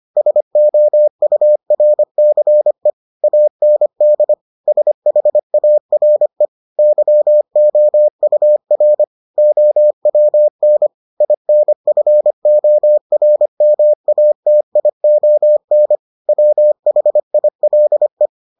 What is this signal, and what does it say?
Morse code, heard as a keyed tone, about 25 wpm, 600 Hz. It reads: SOURCE AND SHARE YOUR OWN INFORMATION WHILE